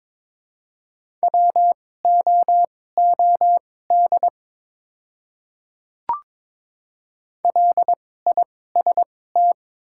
Morse code, heard as a keyed tone, about 22 wpm, spaced as for 16 wpm, 700 Hz. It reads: WOOD LIST
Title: 2x Character Spacing - Top 500 Words - 22wpm